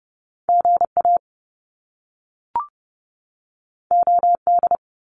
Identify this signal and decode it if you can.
Morse code, heard as a keyed tone, about 30 wpm, 700 Hz. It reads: GA OD
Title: Top Two Letter Combinations in Words 30wpm